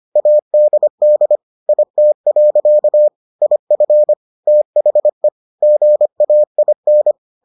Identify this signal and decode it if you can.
Morse code, heard as a keyed tone, about 25 wpm, 600 Hz. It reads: ADD IT. IF THE GAIN